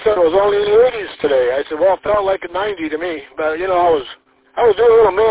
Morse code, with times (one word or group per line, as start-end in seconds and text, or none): none